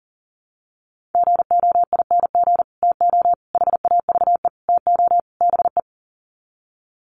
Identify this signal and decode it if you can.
Morse code, heard as a keyed tone, about 40 wpm, 700 Hz. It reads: GOING TO HAVE TO BE
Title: Sets of 5 Words 40wpm